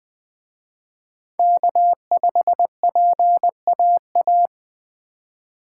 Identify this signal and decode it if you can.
Morse code, heard as a keyed tone, about 20 wpm, 700 Hz. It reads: K5PAA
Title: Speed Racing - USA 1x3 Call Signs 17wpm